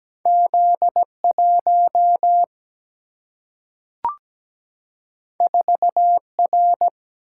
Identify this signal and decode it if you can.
Morse code, heard as a keyed tone, about 17 wpm, 700 Hz. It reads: Z 1 4 R
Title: Single-Letter Single-Number 17wpm